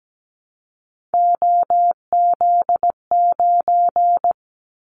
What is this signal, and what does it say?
Morse code, heard as a keyed tone, about 17 wpm, 700 Hz. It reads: OZ9
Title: Callsign Prefixes for Top 50 DXCC - 17wpm